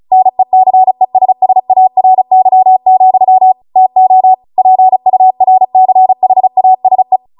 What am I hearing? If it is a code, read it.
Morse code, heard as a keyed tone, about 36 wpm, 750 Hz. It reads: NECESSARY, TO PURCHASE
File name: Morse / 35WPM